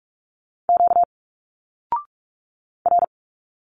Morse code, heard as a keyed tone, about 46 wpm, 700 Hz. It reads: Q R
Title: Single Letters 45wpm